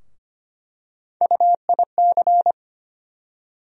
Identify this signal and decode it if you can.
Morse code, heard as a keyed tone, about 25 wpm, 700 Hz. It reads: UIC
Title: USA Call Sign Postfixes - 3 Letters 25wpm